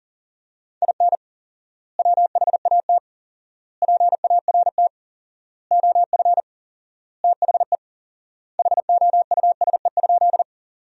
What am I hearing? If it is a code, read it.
Morse code, heard as a keyed tone, about 40 wpm, 700 Hz. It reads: IN WHAT PART OF THE HOUSE?
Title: 4x Word-Spacing - Sentences from Top 200 Words - 40wpm